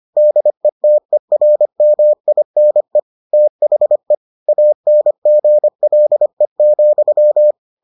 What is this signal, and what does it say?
Morse code, heard as a keyed tone, about 25 wpm, 600 Hz. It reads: DETERMINE THE ANGLE,